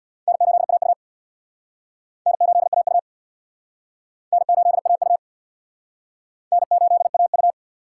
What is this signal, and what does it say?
Morse code, heard as a keyed tone, about 59 wpm, 700 Hz. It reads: N8AU N8AU N8AU N8AU
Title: Speed Racing - USA 1x2 Call Signs 50wpm